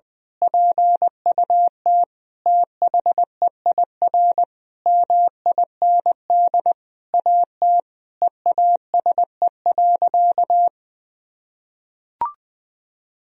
Morse code, heard as a keyed tone, about 20 wpm, 700 Hz. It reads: PUT THEIR MIND AT EASE.
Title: Sentences from Top 1000 Words - Review Part 1 - 20wpm